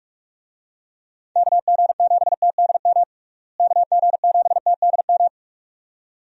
Morse code, heard as a keyed tone, about 44 wpm, 700 Hz. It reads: KG7TDM KG7TDM
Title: USA Call Signs - 6 Characters - Repeated 2x - 45wpm